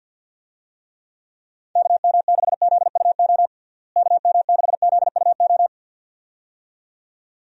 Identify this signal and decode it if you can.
Morse code, heard as a keyed tone, about 52 wpm, 700 Hz. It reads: KM6ZUO KM6ZUO
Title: USA Call Signs - 6 Characters - Repeated 2x - 50wpm